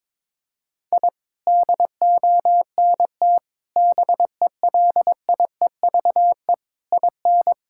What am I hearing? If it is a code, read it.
Morse code, heard as a keyed tone, about 22 wpm, 700 Hz. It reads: I DONT BELIEVE IN
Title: Sentences from Top 1000 Words - Review Part 6 - 22wpm